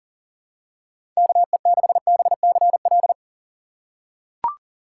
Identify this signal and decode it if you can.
Morse code, heard as a keyed tone, about 37 wpm, 700 Hz. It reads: KE6BCL
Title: USA Call Signs - 6x Character Spacing - 40wpm